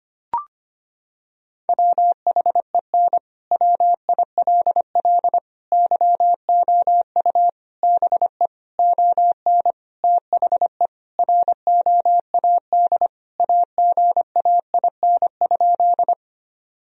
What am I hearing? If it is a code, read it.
Morse code, heard as a keyed tone, about 25 wpm, 700 Hz. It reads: WHEN WILL YOU BE ON THE ROAD AGAIN?
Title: Sentences from Top 1000 Words - Review Part 2 - 25wpm